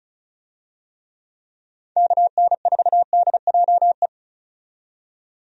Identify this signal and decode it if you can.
Morse code, heard as a keyed tone, about 36 wpm, 700 Hz. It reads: KN4DJE
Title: USA Call Signs - 6 Characters - 35wpm